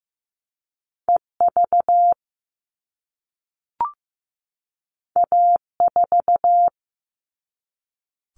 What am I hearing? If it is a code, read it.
Morse code, heard as a keyed tone, about 15 wpm, 700 Hz. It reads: EV A4
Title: All Cut Numbers - 2-Digit 15wpm